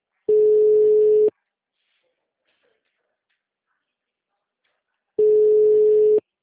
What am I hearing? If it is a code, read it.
Ringback tone (Europe, 425 Hz).
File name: out- -2006- - - .33416